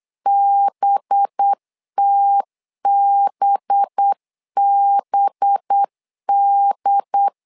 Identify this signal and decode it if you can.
Morse code, heard as a keyed tone, about 8 wpm, 800 Hz. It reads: BTBBD
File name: LEZIONI / 5 LBVC / 5 LB 8wpm 2816